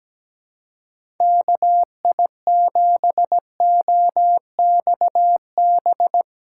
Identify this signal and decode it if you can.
Morse code, heard as a keyed tone, about 17 wpm, 700 Hz. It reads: KI7OXB
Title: USA Call Signs - 6 Characters - 4x Character Spacing - 17wpm